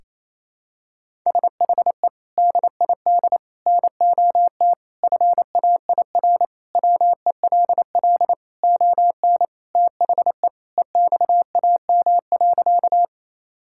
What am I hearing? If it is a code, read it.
Morse code, heard as a keyed tone, about 28 wpm, 700 Hz. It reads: SHE DID NOT FAIR WELL ON THE EXAM.
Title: Sentences from Top 1000 Words - Review Part 5 - 28wpm